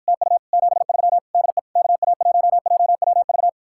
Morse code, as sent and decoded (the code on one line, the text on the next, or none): - ..- / --... ...-- / -.. / -.- .- .---- .--- .-- ...-
TU 73 DE KA1JWV